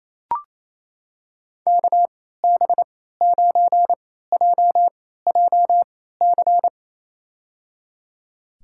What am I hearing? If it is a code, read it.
Morse code, heard as a keyed tone, about 28 wpm, spaced as for 16 wpm, 700 Hz. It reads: KB9JJC
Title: USA Call Signs - 6 Characters - 3x Character Spacing - 28wpm